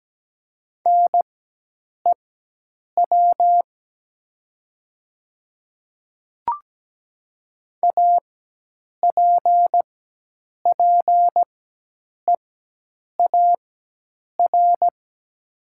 Morse code, heard as a keyed tone, about 17 wpm, spaced as for 8 wpm, 700 Hz. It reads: NEW APPEAR
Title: 4x Character Spacing - Top 500 Words - 17wpm